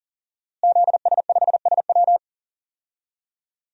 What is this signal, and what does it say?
Morse code, heard as a keyed tone, about 40 wpm, 700 Hz. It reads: ZS5SW
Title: DXCC-100 Call Signs - 5 Characters - Repeated 2x - 40wpm